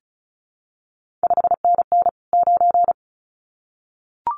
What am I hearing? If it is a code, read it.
Morse code, heard as a keyed tone, about 36 wpm, 700 Hz. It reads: 5NN 9
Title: CQ WW DX Contest Exchange Encore - 35wpm